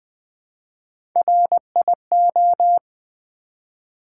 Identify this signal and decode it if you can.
Morse code, heard as a keyed tone, about 20 wpm, 700 Hz. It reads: RIO
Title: Build A Word - 1 Letter At A Time - 4-Letter Words - 20wpm